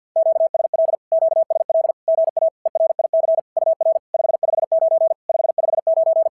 Morse code, heard as a keyed tone, about 52 wpm, 690 Hz. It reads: QSL QSL GA ERIC UR 559 559
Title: SOTA QSO Practice 50wpm